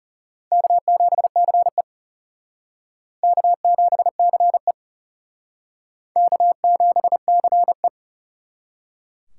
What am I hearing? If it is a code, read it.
Morse code, heard as a keyed tone, about 34 wpm, 700 Hz. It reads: K7CE K7CE K7CE
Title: Speed Racing - USA 1x2 Call Signs 30wpm